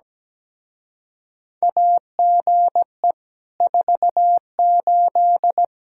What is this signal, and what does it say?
Morse code, heard as a keyed tone, about 17 wpm, 700 Hz. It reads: AGE 48
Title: Speed-Racing Common QSO Phrases - 17wpm